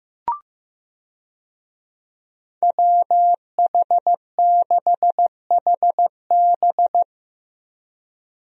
Morse code, heard as a keyed tone, about 15 wpm, 700 Hz. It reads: WH6HB
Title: USA Call Signs - 5 Characters - 15wpm